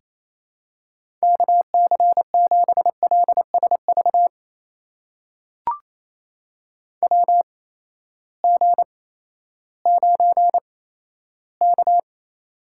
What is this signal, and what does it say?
Morse code, heard as a keyed tone, about 27 wpm, spaced as for 14 wpm, 700 Hz. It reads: KC7LSV WG9K
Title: USA Call Signs - 8x Character Spacing - 28wpm